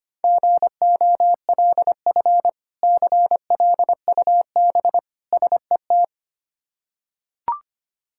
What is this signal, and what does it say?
Morse code, heard as a keyed tone, about 25 wpm, 700 Hz. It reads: GOLF CLUB SET
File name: practice / 25wpm / Difficult Words with Hints - 25wpm